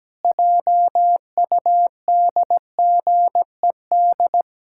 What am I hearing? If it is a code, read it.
Morse code, heard as a keyed tone, about 17 wpm, 700 Hz. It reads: JUDGED
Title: Pangram Challenge v2 - 17wpm